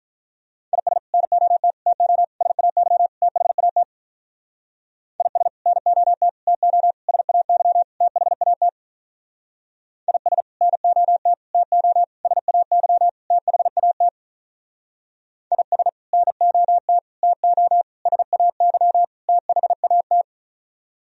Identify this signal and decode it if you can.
Morse code, heard as a keyed tone, about 45 wpm, 700 Hz. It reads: IS NOT TO SAY THAT IS NOT TO SAY THAT IS NOT TO SAY THAT IS NOT TO SAY THAT